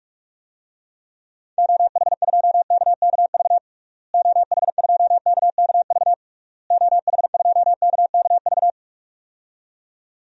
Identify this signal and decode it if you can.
Morse code, heard as a keyed tone, about 41 wpm, 700 Hz. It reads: OH2KKV OH2KKV OH2KKV
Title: DXCC-100 Call Signs - 6 Characters - Repeated 3x - 45wpm